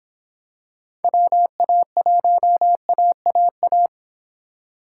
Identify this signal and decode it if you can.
Morse code, heard as a keyed tone, about 26 wpm, 700 Hz. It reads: WA1AAA